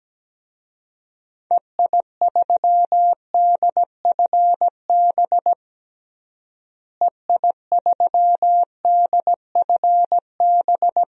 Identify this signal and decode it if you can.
Morse code, heard as a keyed tone, about 17 wpm, 700 Hz. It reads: EI3DFB EI3DFB
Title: DXCC-100 Call Signs - 6 Characters - Repeated 3x - 17wpm